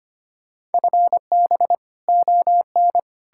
Morse code, heard as a keyed tone, about 33 wpm, 700 Hz. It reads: FB ON